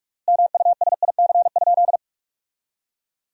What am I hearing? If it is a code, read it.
Morse code, heard as a keyed tone, about 48 wpm, 700 Hz. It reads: MUSIC?
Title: 3x Word-Spacing - Sentences from Top 300 Words - 45wpm